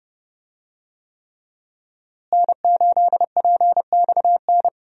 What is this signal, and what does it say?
Morse code, heard as a keyed tone, about 32 wpm, 700 Hz. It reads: N8PXN
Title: USA Call Signs - 5 Characters - Repeated 3x - 30wpm